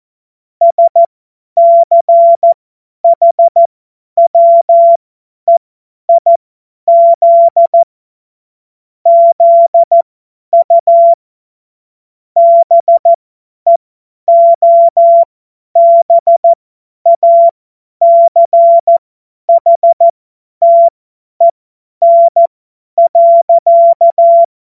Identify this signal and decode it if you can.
Morse code, heard as a keyed tone, about 14 wpm, spaced as for 10 wpm, 680 Hz. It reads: SCHWEIZ ZU BEOBACHTEN.
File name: Morsekurs / Witze / Witz 22. Juni 2021, 50 F70